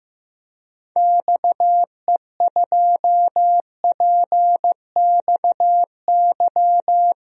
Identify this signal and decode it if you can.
Morse code, heard as a keyed tone, about 16 wpm, 700 Hz. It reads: XE2PXY
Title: DXCC-100 Call Signs - 6 Characters - Repeated 3x - 15wpm